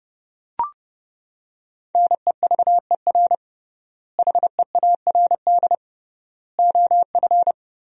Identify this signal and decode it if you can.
Morse code, heard as a keyed tone, about 30 wpm, 700 Hz. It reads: NEVER HEARD OF
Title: 3x Word-Spacing - Sentences from Top 400 Words - 30wpm